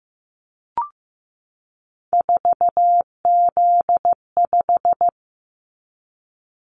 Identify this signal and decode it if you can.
Morse code, heard as a keyed tone, about 15 wpm, 700 Hz. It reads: 4Z5
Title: Callsign Prefixes for Top 50 DXCC - 15wpm